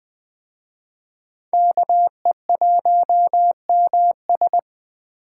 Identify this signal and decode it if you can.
Morse code, heard as a keyed tone, about 20 wpm, 700 Hz. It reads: KE1MS
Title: Speed Racing - USA 2x2 Call Signs 17wpm